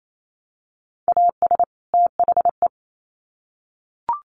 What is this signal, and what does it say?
Morse code, heard as a keyed tone, about 28 wpm, 700 Hz. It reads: AS THE
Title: Sets of 2 Words 28wpm